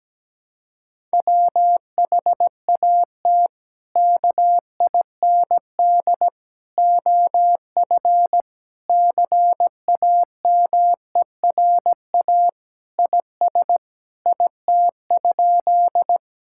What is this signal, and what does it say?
Morse code, heard as a keyed tone, about 17 wpm, 700 Hz. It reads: WHAT KIND OF CAMERA IS IT?
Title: Sentences from Top 1000 Words - Review Part 5 - 17wpm